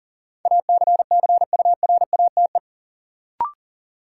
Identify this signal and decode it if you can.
Morse code, heard as a keyed tone, about 40 wpm, 700 Hz. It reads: ACCURATE
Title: Sentences from Top 1200 Words - New Word Drill Alt - 40wpm